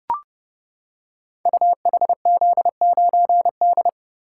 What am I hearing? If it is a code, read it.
Morse code, heard as a keyed tone, about 30 wpm, 700 Hz. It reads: UHZ9D
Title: Code Groups - 5 Letter:Numbers - 30wpm